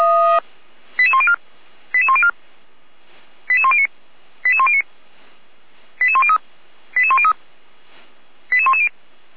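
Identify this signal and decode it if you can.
Five-tone selective call (ZVEI), heard as quick runs of five tones, tones of about 70 ms. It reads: 88104 88104 88109 88109 88103 88103 88110
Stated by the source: ZVEI five-tone selective call 88499 received on SDR, Feuerwehr Orlamünde fire brigade.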